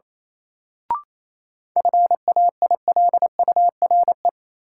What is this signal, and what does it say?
Morse code, heard as a keyed tone, about 28 wpm, 700 Hz. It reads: FAILURE
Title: Sentences from Top 900 Words - New Word Drill - 28wpm